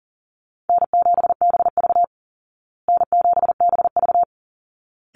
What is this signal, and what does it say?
Morse code, heard as a keyed tone, about 40 wpm, 700 Hz. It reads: N7BV N7BV
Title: Easy Call Signs Repeated 2x - 40wpm